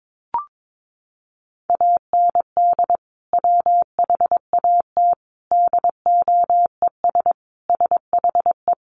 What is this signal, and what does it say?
Morse code, heard as a keyed tone, about 22 wpm, 700 Hz. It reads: AND WHAT DOES SHE
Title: Sentences from Top 400 Words 22wpm